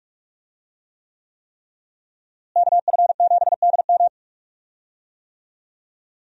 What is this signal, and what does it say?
Morse code, heard as a keyed tone, about 46 wpm, 700 Hz. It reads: KF7DM
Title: USA Call Signs - 5 Characters - Repeated 3x - 45wpm